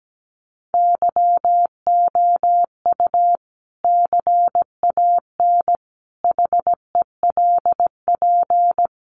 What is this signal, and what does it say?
Morse code, heard as a keyed tone, about 17 wpm, 700 Hz. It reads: YOU CAN HELP
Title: Sentences from Top 200 Words 17wpm